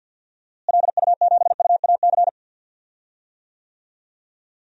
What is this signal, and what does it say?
Morse code, heard as a keyed tone, about 52 wpm, 700 Hz. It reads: LU7UAC
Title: DXCC-100 Call Signs - 6 Characters - Repeated 3x - 50wpm